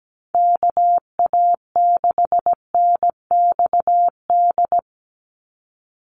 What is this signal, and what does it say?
Morse code, heard as a keyed tone, about 17 wpm, 700 Hz. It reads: KA6NXD
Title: USA Call Signs 17wpm